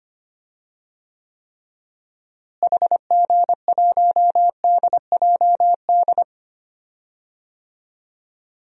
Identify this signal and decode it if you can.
Morse code, heard as a keyed tone, about 25 wpm, 700 Hz. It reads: HG1DJD
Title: DXCC-100 Call Signs - 6 Characters - Repeated 3x - 25wpm